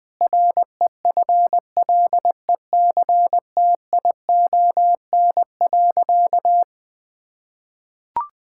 Morse code, heard as a keyed tone, about 20 wpm, 700 Hz. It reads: REFLECTION.
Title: Difficult Sentences with Hints - 20wpm